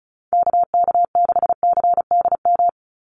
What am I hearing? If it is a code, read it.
Morse code, heard as a keyed tone, about 34 wpm, 700 Hz. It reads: KK6CDM
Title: USA Call Signs 35wpm